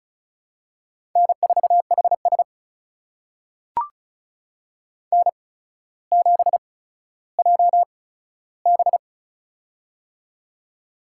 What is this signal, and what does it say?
Morse code, heard as a keyed tone, about 36 wpm, spaced as for 15 wpm, 700 Hz. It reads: N4HS N7JB
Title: USA Call Signs - 4 Characters - 8x Character Spacing - 35wpm